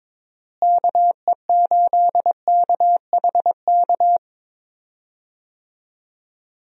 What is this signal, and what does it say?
Morse code, heard as a keyed tone, about 22 wpm, 700 Hz. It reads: KE8KHK
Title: USA Call Signs - 6 Characters - Repeated 3x - 22wpm